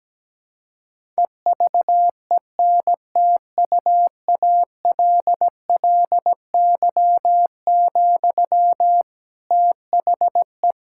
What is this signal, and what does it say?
Morse code, heard as a keyed tone, about 17 wpm, 700 Hz. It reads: EVENTUALLY, THE
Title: Sentences from Top 1000 Words - Review Part 6 - 17wpm